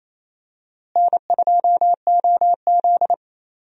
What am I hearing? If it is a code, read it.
Morse code, heard as a keyed tone, about 28 wpm, 700 Hz. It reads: N2OZ
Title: USA Call Signs - 4 Characters - Repeated 2x - 28wpm